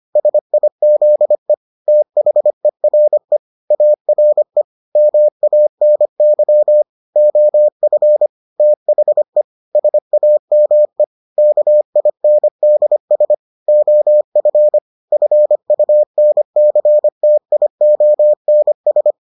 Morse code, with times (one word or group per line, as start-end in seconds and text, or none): 0.2-1.5 SIZE
1.9-3.4 THERE
3.7-4.6 ARE
5.0-6.8 MANY
7.2-8.3 OF
8.6-9.4 THE
9.8-11.0 SAME
11.4-13.3 KINDS
13.7-14.8 OF
15.1-19.1 FUNCTIONS